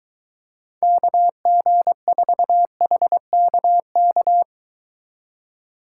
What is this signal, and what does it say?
Morse code, heard as a keyed tone, about 24 wpm, 700 Hz. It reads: KG4HKK